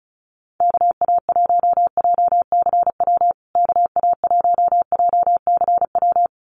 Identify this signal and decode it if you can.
Morse code, heard as a keyed tone, about 34 wpm, 700 Hz. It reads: KA1JCW KA1JCW
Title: USA Call Signs Repeated 2x 35wpm